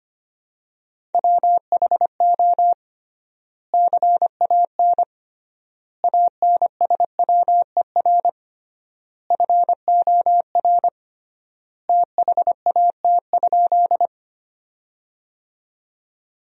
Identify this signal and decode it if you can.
Morse code, heard as a keyed tone, about 25 wpm, 700 Hz. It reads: WHO CAN ANSWER FOR THAT?
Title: 3x Word-Spacing - Sentences from Top 300 Words - 25wpm